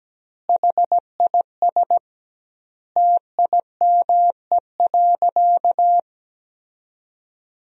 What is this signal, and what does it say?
Morse code, heard as a keyed tone, about 17 wpm, 700 Hz. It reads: HIS TIME.
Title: 2x Word-Spacing - Sentences from Top 400 Words - 17wpm